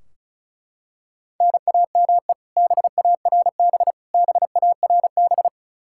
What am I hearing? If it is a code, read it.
Morse code, heard as a keyed tone, about 36 wpm, 700 Hz. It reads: NAME BARB BARB
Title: Practice QSO - Exchange 1 - Beginning - 35wpm